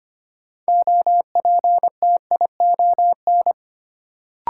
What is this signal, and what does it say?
Morse code, heard as a keyed tone, about 25 wpm, 700 Hz. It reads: OPTION
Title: Sentences from Top 1300 Words - New Word Drill Alt - 25wpm